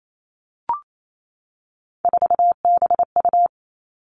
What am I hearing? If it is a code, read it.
Morse code, heard as a keyed tone, about 28 wpm, 700 Hz. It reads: 4BU
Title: Three Challenging Letter-Number Combinations 28wpm